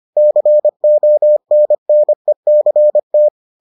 Morse code, heard as a keyed tone, about 25 wpm, 600 Hz. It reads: CONNECT